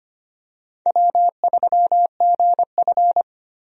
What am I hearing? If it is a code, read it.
Morse code, heard as a keyed tone, about 25 wpm, 700 Hz. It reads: W3GF
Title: USA Call Signs - 4 Characters - 25wpm